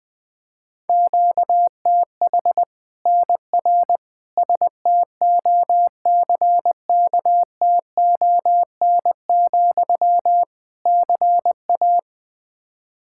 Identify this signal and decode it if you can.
Morse code, heard as a keyed tone, about 20 wpm, 700 Hz. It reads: QTH NR STOCKTON, CA
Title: Practice QSO - Exchange 1 - Beginning - 20wpm